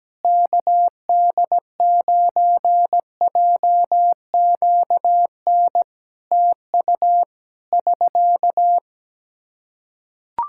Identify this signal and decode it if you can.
Morse code, heard as a keyed tone, about 17 wpm, 700 Hz. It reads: KD9JQN TU <SK>
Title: Practice QSO - Exchange 4 - Ending - 17wpm